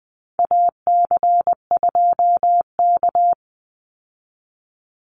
Morse code, heard as a keyed tone, about 20 wpm, 700 Hz. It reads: AC2K
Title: Call Signs Repeated 3x - 20wpm